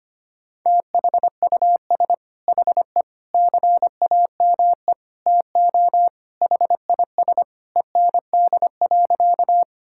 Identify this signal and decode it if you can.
Morse code, heard as a keyed tone, about 25 wpm, 700 Hz. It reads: THUS HE CAME TO HIS END.
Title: Sentences from Top 1000 Words - Review Part 5 - 25wpm